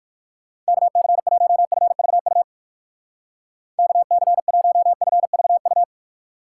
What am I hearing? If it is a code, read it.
Morse code, heard as a keyed tone, about 57 wpm, 700 Hz. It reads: KC1FVU KC1FVU